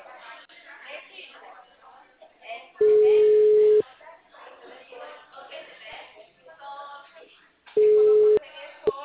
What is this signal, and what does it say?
Ringback tone (Europe, 425 Hz), with speech.